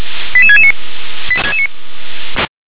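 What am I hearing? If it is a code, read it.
Five-tone selective call (ZVEI), heard as quick runs of five tones, tones of about 70 ms. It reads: 88669
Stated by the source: ZVEI five-tone selective call received with an SDR.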